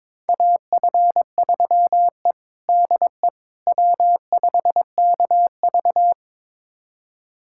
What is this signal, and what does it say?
Morse code, heard as a keyed tone, about 22 wpm, 700 Hz. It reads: AF3E DE W5KV
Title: USA Callsign DE Callsign - 4 Characters - 22wpm